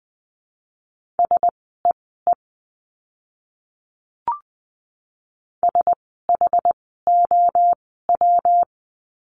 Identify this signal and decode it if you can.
Morse code, heard as a keyed tone, about 20 wpm, spaced as for 15 wpm, 700 Hz. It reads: SEE SHOW
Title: 230 Lesson 37f - Letter x - Words 20wpm